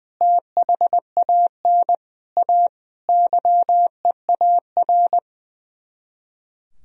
Morse code, heard as a keyed tone, about 21 wpm, 700 Hz. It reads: THAN A YEAR
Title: Speed-Racing - Sets of 5 Words 20wpm